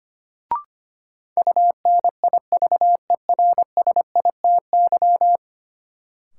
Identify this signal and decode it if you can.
Morse code, heard as a keyed tone, about 25 wpm, 700 Hz. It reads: UNIVERSITY
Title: Sentences from Top 1000 Words - Review Part 4 - New Word Drill - 25wpm